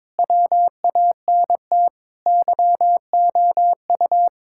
Morse code, heard as a keyed tone, about 22 wpm, 700 Hz. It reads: WANT YOU
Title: Sentences from Top 1000 Words - Review Part 5 - 22wpm